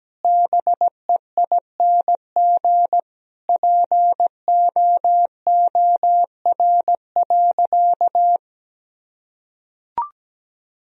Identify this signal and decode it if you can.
Morse code, heard as a keyed tone, about 17 wpm, 700 Hz. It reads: BEING POOR.
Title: Sentences from Top 1000 Words - Review Part 3 - 17wpm